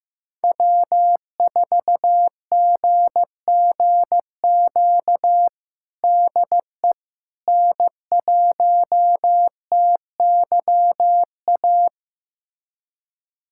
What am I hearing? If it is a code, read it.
Morse code, heard as a keyed tone, about 15 wpm, 700 Hz. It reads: W4GGQ DE N1TYA
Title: USA Callsign DE Callsign - 5 Characters - 15wpm